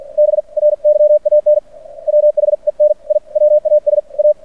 Morse code, heard as a keyed tone, about 50 wpm, spaced as for 44 wpm, 590 Hz. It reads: BRYAN WHEN I WAS A
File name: archived / CRSnet